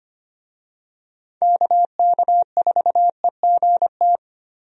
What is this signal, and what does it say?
Morse code, heard as a keyed tone, about 25 wpm, 700 Hz. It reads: KK4EGT
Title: USA Call Signs - 6 Characters - Repeated 3x - 25wpm